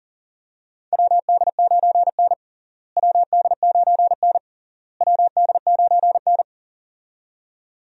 Morse code, heard as a keyed tone, about 40 wpm, 700 Hz. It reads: WD9N WD9N WD9N
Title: USA Call Signs - 4 Characters - Repeated 3x - 40wpm